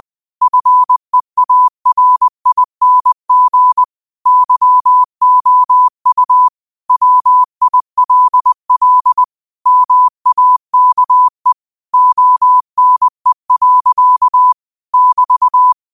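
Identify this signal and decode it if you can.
Morse code, heard as a keyed tone, about 20 wpm, 1000 Hz. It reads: FEARING YOU WILL MAKE ONE. <BT>
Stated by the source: Morse code practice at 20 wpm: famous quotations sent in telegraphy, from URVAG.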